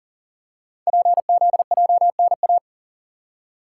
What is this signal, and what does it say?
Morse code, heard as a keyed tone, about 42 wpm, 700 Hz. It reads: PZJNA
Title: Code Groups - 5 Letters - 40wpm